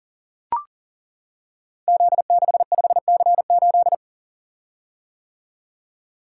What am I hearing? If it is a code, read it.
Morse code, heard as a keyed tone, about 43 wpm, 690 Hz. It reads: Z65C8
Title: Code Groups - 5 Letter:Numbers - 40wpm